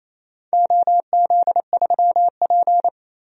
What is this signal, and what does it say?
Morse code, heard as a keyed tone, about 28 wpm, 700 Hz. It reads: OZ3P
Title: DXCC-100 Call Signs - 4 Characters - 28wpm